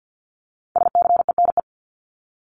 Morse code, heard as a keyed tone, about 50 wpm, 700 Hz. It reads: SCENE
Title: Top 1001-1250 Words 50wpm